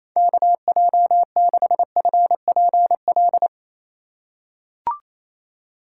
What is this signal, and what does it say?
Morse code, heard as a keyed tone, about 27 wpm, 700 Hz. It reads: KJ6FPL
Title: USA Call Signs - 3x Character Spacing - 28wpm